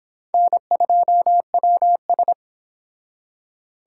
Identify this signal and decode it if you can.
Morse code, heard as a keyed tone, about 26 wpm, 700 Hz. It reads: N2WS